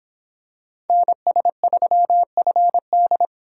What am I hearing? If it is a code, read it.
Morse code, heard as a keyed tone, about 26 wpm, 700 Hz. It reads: NS3FD